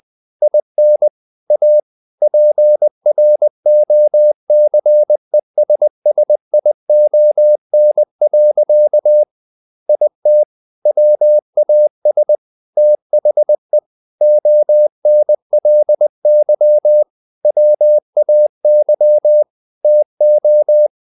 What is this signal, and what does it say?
Morse code, heard as a keyed tone, about 20 wpm, 600 Hz. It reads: IN A PROCESSION. IT WAS THE ONLY WAY TO
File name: the old man and the sea part 2 20wpm